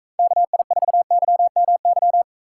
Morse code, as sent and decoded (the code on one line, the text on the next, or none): -.- .. ....- -.-- -- -.--
KI4YMY